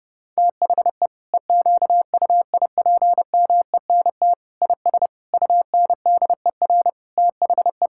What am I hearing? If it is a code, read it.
Morse code, heard as a keyed tone, about 30 wpm, 700 Hz. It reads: THE EQUIPMENT IS UNDER THE
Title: Sentences from Top 1000 Words - Review Part 5 - 30wpm